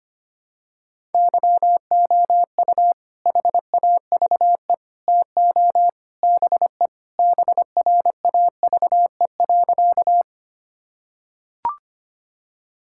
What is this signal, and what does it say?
Morse code, heard as a keyed tone, about 25 wpm, 700 Hz. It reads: YOU HAVE TO BE BRAVE.
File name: practice / 25wpm / Sentences with 3-Word Phrases - 25wpm